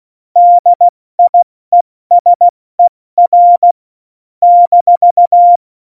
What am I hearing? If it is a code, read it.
Morse code, heard as a keyed tone, about 16 wpm, 700 Hz. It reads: DIESER -
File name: Morsekurs / Witze / Witz 19 Mai, 2020 70 F80